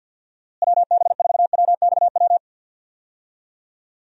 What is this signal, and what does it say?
Morse code, heard as a keyed tone, about 49 wpm, 700 Hz. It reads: WB4WXW
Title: USA Call Signs - 6 Characters - Repeated 2x - 50wpm